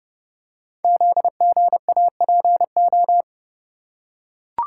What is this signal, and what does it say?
Morse code, heard as a keyed tone, about 30 wpm, 700 Hz. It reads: ZGAPO
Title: Code Groups - 5 Letters - 30wpm